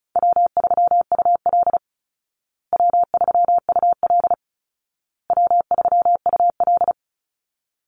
Morse code, heard as a keyed tone, about 34 wpm, 700 Hz. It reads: W3UL W3UL W3UL
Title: Easy Call Signs Repeated 3x - 35wpm